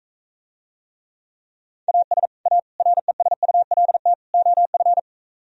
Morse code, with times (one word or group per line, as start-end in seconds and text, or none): 1.9-2.3 AS
2.5-2.6 A
2.8-4.1 RESULT
4.3-5.0 OF